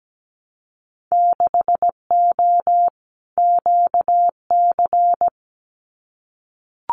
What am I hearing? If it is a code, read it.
Morse code, heard as a keyed tone, about 17 wpm, 700 Hz. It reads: 6O QC
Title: Winter Field Day Exchanges Encore 17wpm